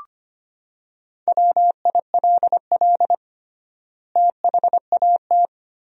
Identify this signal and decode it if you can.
Morse code, heard as a keyed tone, about 25 wpm, 700 Hz. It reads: WILL THAT
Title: 3x Word-Spacing - Sentences from Top 100 Words - 25wpm